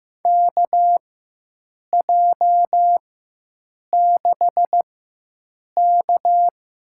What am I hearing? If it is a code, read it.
Morse code, heard as a keyed tone, about 15 wpm, spaced as for 7 wpm, 700 Hz. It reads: KJ6K
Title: USA Call Signs - 4x Character Spacing - 15wpm